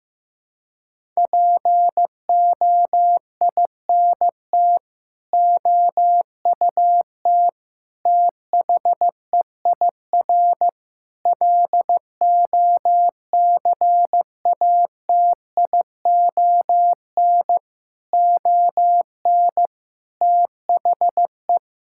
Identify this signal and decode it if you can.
Morse code, heard as a keyed tone, about 15 wpm, 700 Hz. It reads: POINT OUT THEIR LOCATION ON THE
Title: Sentences from Top 1000 Words - Review Part 6 - 15wpm